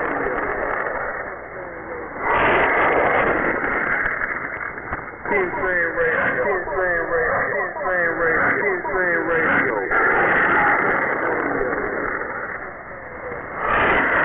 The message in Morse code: none